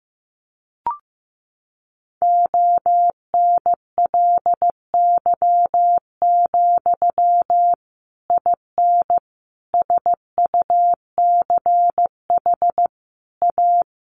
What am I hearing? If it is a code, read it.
Morse code, heard as a keyed tone, about 15 wpm, 700 Hz. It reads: ONLY, IN SUCH A
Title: Sentences from Top 1500 Words 15wpm